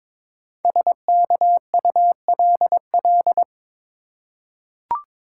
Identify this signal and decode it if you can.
Morse code, heard as a keyed tone, about 22 wpm, 700 Hz. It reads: SKULL
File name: 5 Letter Words - First 3 Letters Given - 22wpm